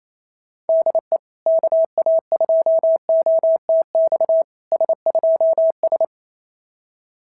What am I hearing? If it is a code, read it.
Morse code, heard as a keyed tone, about 27 wpm, 650 Hz. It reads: DE KA2OTX S2S
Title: SOTA QSO Practice 28wpm